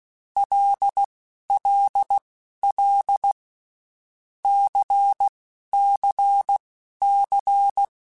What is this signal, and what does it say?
Morse code, heard as a keyed tone, about 16 wpm, spaced as for 11 wpm, 800 Hz. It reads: LLL CCC
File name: Groups@10 / Groups@10 mp3s / LFCYQ@10